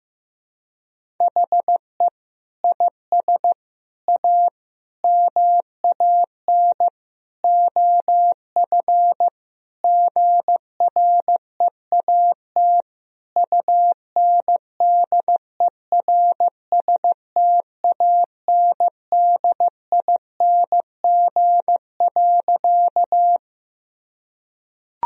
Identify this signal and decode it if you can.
Morse code, heard as a keyed tone, about 15 wpm, 700 Hz. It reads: HE IS A MAN OF GREAT UNDERSTANDING.
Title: Sentences from Top 1000 Words - Review Part 3 - 15wpm